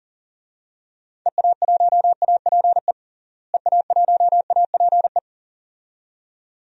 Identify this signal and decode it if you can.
Morse code, heard as a keyed tone, about 39 wpm, 700 Hz. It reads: EA1APE EA1APE
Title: DXCC-100 Call Signs - 6 Characters - Repeated 2x - 40wpm